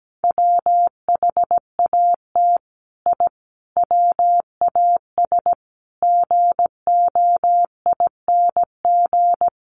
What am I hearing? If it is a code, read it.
Morse code, heard as a keyed tone, about 17 wpm, 700 Hz. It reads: WHAT I WAS GOING
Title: Sets of 5 Words Encore2 17wpm